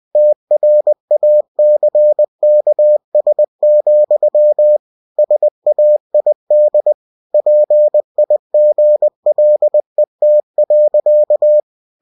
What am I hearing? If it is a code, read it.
Morse code, heard as a keyed tone, about 20 wpm, 600 Hz. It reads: TRACKS, SAID PIGLET.
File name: winnie-20wpm